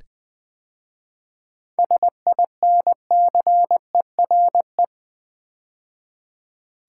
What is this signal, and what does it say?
Morse code, heard as a keyed tone, about 20 wpm, 700 Hz. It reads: SINCERE
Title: 7 Letter Words - First 3 Letters Given - 20wpm